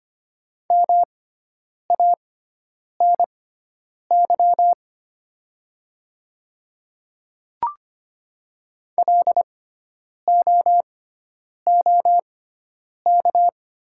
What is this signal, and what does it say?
Morse code, heard as a keyed tone, about 25 wpm, spaced as for 9 wpm, 690 Hz. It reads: MANY LOOK
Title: 6x Character Spacing - Top 100 Words - 25wpm